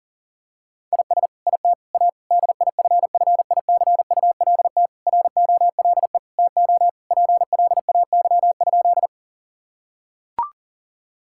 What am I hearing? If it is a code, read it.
Morse code, heard as a keyed tone, about 40 wpm, 700 Hz. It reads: IS IT A DIFFICULT ROLE TO PLAY?
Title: Sentences from Top 1000 Words - Review Part 6 - 40wpm